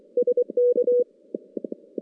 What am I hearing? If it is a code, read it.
Morse code, heard as a keyed tone, about 23 wpm, 490 Hz. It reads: SK
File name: contests / CW-SS-k5tr- -0000